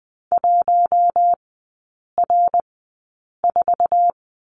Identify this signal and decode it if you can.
Morse code, heard as a keyed tone, about 20 wpm, spaced as for 8 wpm, 700 Hz. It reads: 1R4
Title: 053 Lesson 10e - Letter r - Three Characters 20wpm